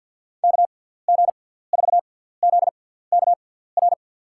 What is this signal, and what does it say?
Morse code, heard as a keyed tone, about 52 wpm, spaced as for 17 wpm, 700 Hz. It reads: KG4ZKR
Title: USA Call Signs - 6x Character Spacing - 50wpm